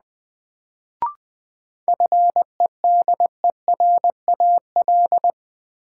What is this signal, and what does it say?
Morse code, heard as a keyed tone, about 20 wpm, 700 Hz. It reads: FEDERAL
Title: Sets of 3 Words - New Word Drill - 20wpm